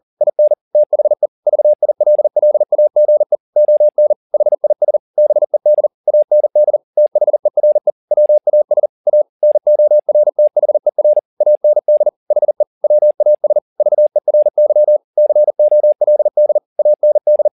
Morse code, heard as a keyed tone, about 40 wpm, 600 Hz. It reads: IN THE VILLAGE ON HIS BED AND THERE WAS A NORTHER AND HE WAS VERY COLD AND
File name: the old man and the sea 2 40wpm